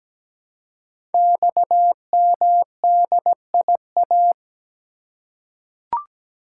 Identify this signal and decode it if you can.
Morse code, heard as a keyed tone, about 17 wpm, 700 Hz. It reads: XMDIA
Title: Code Groups - 5 Letters - 17wpm